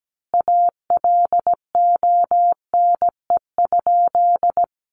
Morse code, heard as a keyed tone, about 17 wpm, 700 Hz. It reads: ALONE?
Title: Sentences from Top 1000 Words 17wpm